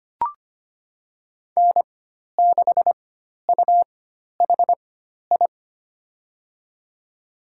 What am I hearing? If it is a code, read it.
Morse code, heard as a keyed tone, about 25 wpm, spaced as for 12 wpm, 700 Hz. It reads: N6UHI
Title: USA Call Signs - 5 Characters - 4x Character Spacing - 25wpm